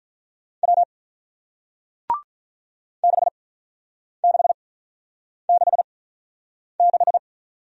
Morse code, heard as a keyed tone, about 47 wpm, spaced as for 9 wpm, 700 Hz. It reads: W 6666